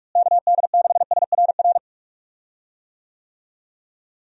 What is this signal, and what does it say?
Morse code, heard as a keyed tone, about 41 wpm, 700 Hz. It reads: KD6SRR
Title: USA Call Signs - 6 Characters - Repeated 3x - 45wpm